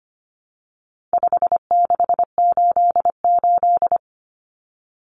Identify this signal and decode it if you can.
Morse code, heard as a keyed tone, about 25 wpm, 700 Hz. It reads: 5688
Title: Common Cut Numbers - 4-Digit 25wpm